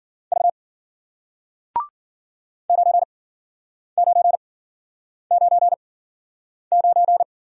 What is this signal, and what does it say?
Morse code, heard as a keyed tone, about 53 wpm, spaced as for 9 wpm, 700 Hz. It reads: V 9999